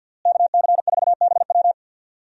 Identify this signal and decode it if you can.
Morse code, heard as a keyed tone, about 52 wpm, 700 Hz. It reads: KC4BW
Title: USA Call Signs - 5 Characters - Repeated 3x - 50wpm